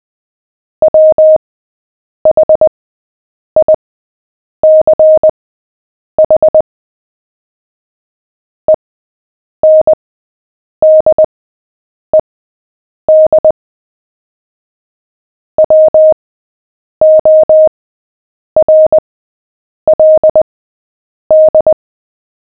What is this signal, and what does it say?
Morse code, heard as a keyed tone, about 20 wpm, spaced as for 8 wpm, 620 Hz. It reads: WHICH ENDED WORLD
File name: cw / fists / Short-Story-7-8wpm